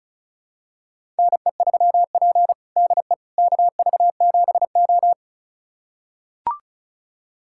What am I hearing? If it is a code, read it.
Morse code, heard as a keyed tone, about 36 wpm, 700 Hz. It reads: NE3P DE KV7O
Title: USA Callsign DE Callsign - 4 Characters - 35wpm